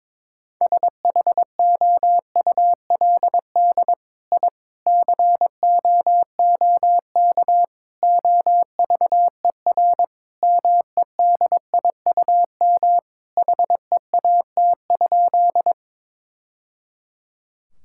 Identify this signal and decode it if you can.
Morse code, heard as a keyed tone, about 21 wpm, 700 Hz. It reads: SHOULD I COOK OVER MEDIUM HEAT?
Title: Sentences from Top 1000 Words - Review Part 6 - 22wpm